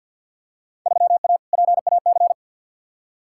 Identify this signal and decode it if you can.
Morse code, heard as a keyed tone, about 47 wpm, 700 Hz. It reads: 3A PAC
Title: Field Day Exchanges - 50wpm